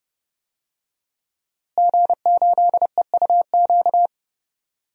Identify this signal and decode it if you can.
Morse code, heard as a keyed tone, about 30 wpm, 700 Hz. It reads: G8EUQ
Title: DXCC-100 Call Signs - 5 Characters - Repeated 3x - 30wpm